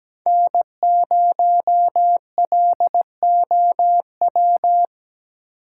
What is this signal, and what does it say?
Morse code, heard as a keyed tone, about 17 wpm, 700 Hz. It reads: N0LOW
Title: USA Call Signs - 5 Characters - 3x Character Spacing - 17wpm